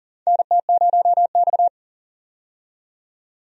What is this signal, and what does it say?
Morse code, heard as a keyed tone, about 40 wpm, 700 Hz. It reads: NT0X
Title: USA Call Signs - 4 Characters - 40wpm